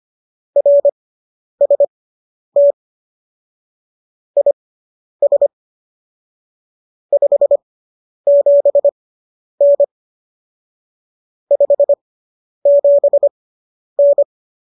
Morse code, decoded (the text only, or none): RST IS 57N 57N